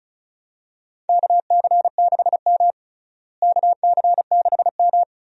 Morse code, heard as a keyed tone, about 36 wpm, 700 Hz. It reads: KC6M KC6M
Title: USA Call Signs - 4 Characters - Repeated 3x - 35wpm